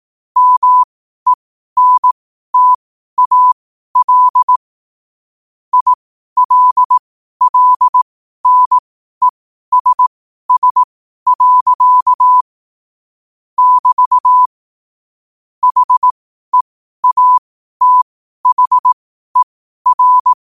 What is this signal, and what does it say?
Morse code, heard as a keyed tone, about 18 wpm, spaced as for 12 wpm, 1000 Hz. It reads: MENTAL ILLNESS. <BT> HEATHER
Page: New quotes every day in morse code at 12 Words per minute.